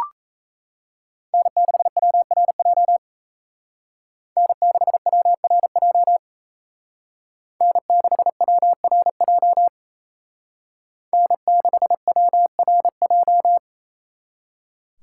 Morse code, heard as a keyed tone, about 36 wpm, 700 Hz. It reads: N6WRJ N6WRJ N6WRJ N6WRJ